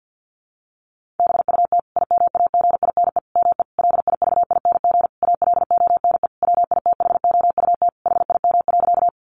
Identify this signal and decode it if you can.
Morse code, heard as a keyed tone, about 50 wpm, 700 Hz. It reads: BUT IMAGINE ME LIVING ALONE WITHOUT HIM.
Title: Sentences from Top 1400 Words 50wpm